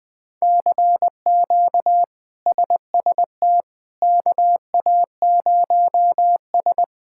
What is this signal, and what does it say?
Morse code, heard as a keyed tone, about 21 wpm, 700 Hz. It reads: CQ SST KA0S
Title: K1USN SST QSO Practice - 20wpm